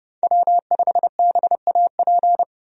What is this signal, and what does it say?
Morse code, heard as a keyed tone, about 30 wpm, 700 Hz. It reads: W5BAP
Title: USA Call Signs - 5 Characters - 30wpm